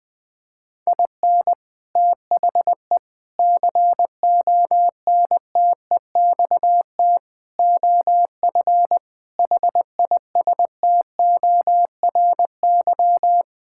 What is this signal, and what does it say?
Morse code, heard as a keyed tone, about 20 wpm, 700 Hz. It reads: IN THE CONTEXT OF HISTORY
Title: Sentences with 4-Word Phrases - 20wpm